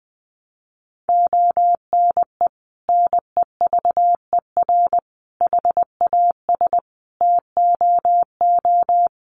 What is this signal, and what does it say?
Morse code, heard as a keyed tone, about 20 wpm, 700 Hz. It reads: ONE NEVER HAS TOO
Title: Sentences from Top 300 Words 20wpm